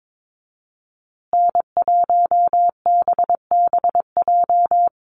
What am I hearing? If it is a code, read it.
Morse code, heard as a keyed tone, about 22 wpm, 700 Hz. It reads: N1BBJ
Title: USA Call Signs Repeated 3x 22wpm